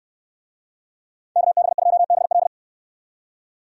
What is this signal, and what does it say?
Morse code, heard as a keyed tone, about 69 wpm, 700 Hz. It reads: KB2BL